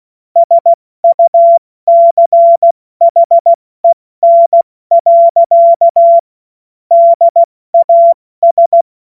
Morse code, heard as a keyed tone, about 16 wpm, 680 Hz. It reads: SUCHEN. DAS
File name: Morsekurs / Witze / Witz 20. Januar 2026 70 F80